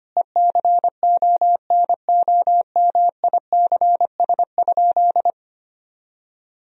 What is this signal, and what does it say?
Morse code, heard as a keyed tone, about 25 wpm, 700 Hz. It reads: ECONOMICS?
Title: Sentences from Top 1000 Words - Review Part 3 - 25wpm